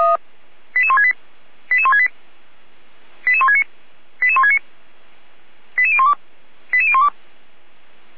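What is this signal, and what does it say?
Five-tone selective call (ZVEI), heard as quick runs of five tones, tones of about 70 ms. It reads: 88168 88168 88169 88169 88012 88012
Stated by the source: ZVEI five-tone selective call received by SDR.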